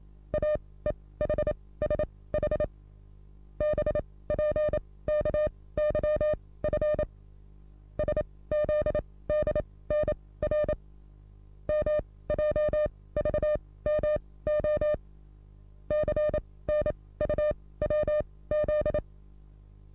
Morse code, heard as a keyed tone, about 27 wpm, spaced as for 18 wpm, 600 Hz. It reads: AEHSH BPKYF SZDNR MJVMO CNUWZ
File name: repos / Morsekurs / DJ4UF Morsen27